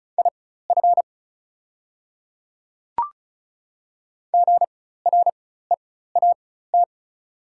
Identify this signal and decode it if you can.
Morse code, heard as a keyed tone, about 36 wpm, spaced as for 16 wpm, 690 Hz. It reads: IF GREAT
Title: 4x Character Spacing - Top 300 Words - 35wpm